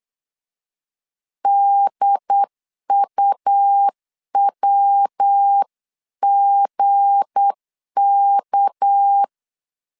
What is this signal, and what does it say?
Morse code, heard as a keyed tone, about 8 wpm, 800 Hz. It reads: DUWGK
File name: CW / DET / 1 9 1500C S A 8wpm 5637